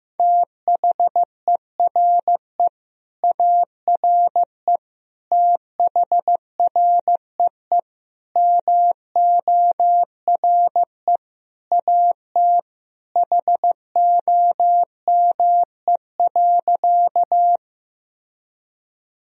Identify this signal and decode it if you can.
Morse code, heard as a keyed tone, about 15 wpm, 700 Hz. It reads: THERE ARE THREE MORE AT HOME.
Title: Sentences from Top 1000 Words - Review Part 1 - 15wpm